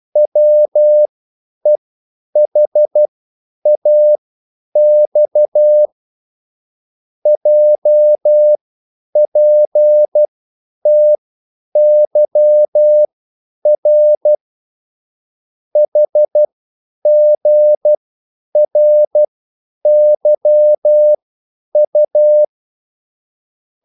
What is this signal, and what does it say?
Morse code, heard as a keyed tone, about 12 wpm, spaced as for 9 wpm, 600 Hz. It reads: WEHAX JPTYR HGRYU